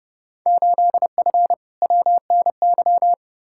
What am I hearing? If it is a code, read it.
Morse code, heard as a keyed tone, about 30 wpm, 700 Hz. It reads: 8F WNY
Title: Field Day Exchanges Encore - 30wpm